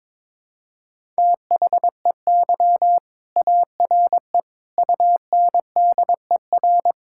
Morse code, heard as a keyed tone, about 22 wpm, 700 Hz. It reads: THEY ARE UNDER
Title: Sentences from Top 1000 Words - Review Part 6 - 22wpm